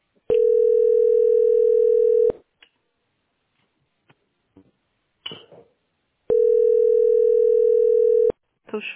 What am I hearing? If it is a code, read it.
Ringback tone (North America), 440 + 480 Hz, with speech.
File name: 4d3f725b-93e9-43fe-b86e